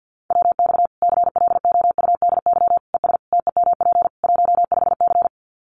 Morse code, heard as a keyed tone, about 50 wpm, 700 Hz. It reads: WX CLOUDY ES TEMP 15C
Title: Common QSO Phrases 50wpm